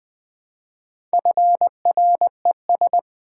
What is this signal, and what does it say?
Morse code, heard as a keyed tone, about 20 wpm, 700 Hz. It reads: FRES
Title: Build A Word - 2 Letters At A Time - 8-Letter Words - 20wpm